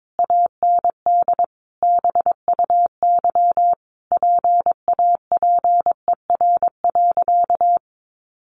Morse code, heard as a keyed tone, about 22 wpm, 700 Hz. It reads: AND BUY PAPER.
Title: Sentences from Top 1200 Words 22wpm